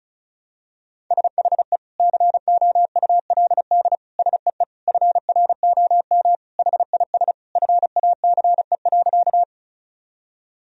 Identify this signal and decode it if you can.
Morse code, heard as a keyed tone, about 36 wpm, 700 Hz. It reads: SHE COULD SEE FROM HIS FACE.
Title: Sentences from Top 1000 Words - Review Part 2 - 35wpm